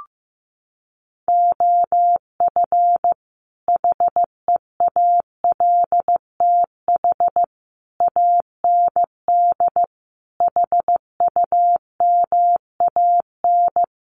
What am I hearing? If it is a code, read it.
Morse code, heard as a keyed tone, about 15 wpm, 700 Hz. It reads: OF HEALTH AND HUMAN
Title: Sets of 5 Words 15wpm